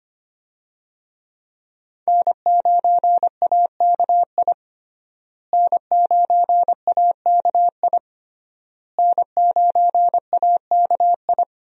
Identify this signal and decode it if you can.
Morse code, heard as a keyed tone, about 25 wpm, 700 Hz. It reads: N9AKI N9AKI N9AKI
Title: USA Call Signs - 5 Characters - Repeated 3x - 25wpm